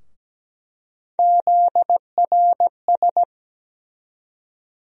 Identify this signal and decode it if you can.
Morse code, heard as a keyed tone, about 17 wpm, 700 Hz. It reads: ZRS
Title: USA Call Sign Postfixes - 3 Letters 17wpm